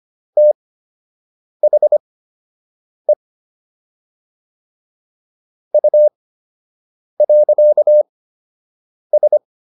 Morse code, heard as a keyed tone, about 25 wpm, spaced as for 7 wpm, 600 Hz. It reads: THE U.S